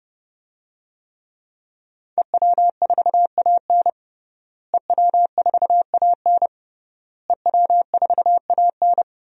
Morse code, heard as a keyed tone, about 30 wpm, 700 Hz. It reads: EW4AN EW4AN EW4AN
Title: DXCC-100 Call Signs - 5 Characters - Repeated 3x - 30wpm